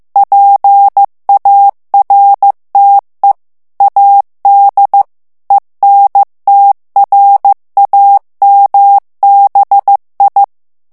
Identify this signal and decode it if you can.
Morse code, heard as a keyed tone, about 15 wpm, 800 Hz. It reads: PARTE AD ENTRAMBI
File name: morse mp3 / ITALIEN texte clair 2 vitesse 14 mots minute